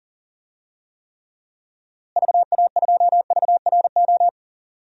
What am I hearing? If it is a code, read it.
Morse code, heard as a keyed tone, about 40 wpm, 700 Hz. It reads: VA2VRO
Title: DXCC-100 Call Signs - 6 Characters - Repeated 2x - 40wpm